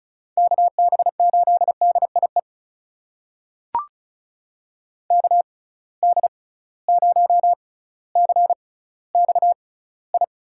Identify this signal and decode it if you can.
Morse code, heard as a keyed tone, about 36 wpm, spaced as for 18 wpm, 700 Hz. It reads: KB8DIE KD0CXI
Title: USA Call Signs - 6 Characters - 6x Character Spacing - 35wpm